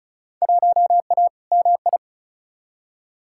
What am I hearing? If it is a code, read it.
Morse code, heard as a keyed tone, about 34 wpm, 700 Hz. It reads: 1A MI
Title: Field Day Exchanges - 35wpm